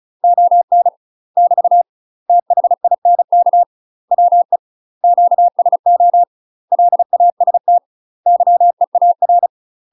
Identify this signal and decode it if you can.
Morse code, heard as a keyed tone, about 34 wpm, 700 Hz. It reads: ON <BT> THINK WE QSO LAST YEAR
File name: UK MASTER QSOs TRAINING / 35WPM / UK MASTER QSOs 0030